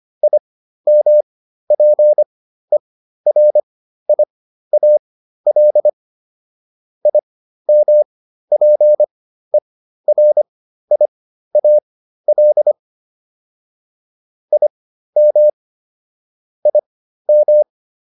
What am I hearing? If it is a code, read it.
Morse code, heard as a keyed tone, about 25 wpm, spaced as for 13 wpm, 600 Hz. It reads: IMPERIAL IMPERIAL IM IM